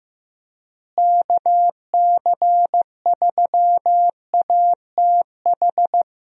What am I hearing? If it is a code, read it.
Morse code, heard as a keyed tone, about 15 wpm, 700 Hz. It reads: KC3ATH
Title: USA Call Signs - 6 Characters - Repeated 2x - 15wpm